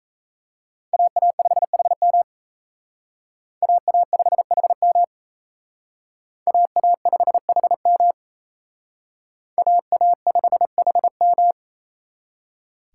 Morse code, heard as a keyed tone, about 34 wpm, 700 Hz. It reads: AA5HM AA5HM AA5HM AA5HM
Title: Speed Racing - USA 2x2 Call Signs 28wpm